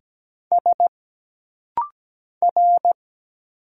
Morse code, heard as a keyed tone, about 17 wpm, 700 Hz. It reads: S R
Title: Single Letters - Rapid-Fire 17wpm